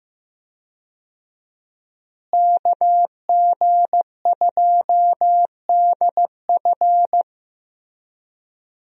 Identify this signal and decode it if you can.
Morse code, heard as a keyed tone, about 16 wpm, 700 Hz. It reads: KG2DF
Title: USA Call Signs - 5 Characters - Repeated 3x - 15wpm